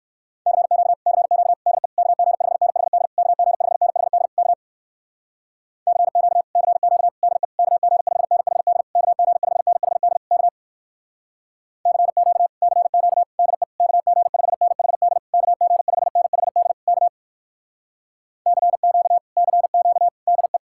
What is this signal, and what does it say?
Morse code, heard as a keyed tone, about 69 wpm, 700 Hz. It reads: CQ CQ DE KG5NHD KG5NHD K CQ CQ DE KG5NHD KG5NHD K CQ CQ DE KG5NHD KG5NHD K CQ CQ DE